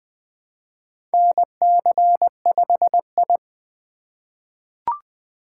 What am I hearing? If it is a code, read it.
Morse code, heard as a keyed tone, about 20 wpm, 700 Hz. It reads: NC5I
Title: USA Call Signs - 4 Characters - 2x Character Spacing - 20wpm